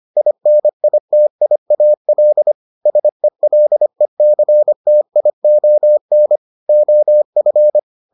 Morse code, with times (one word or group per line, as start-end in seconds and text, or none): 0.2-2.5 INITIAL
2.9-6.4 SELECTION
6.7-7.8 OF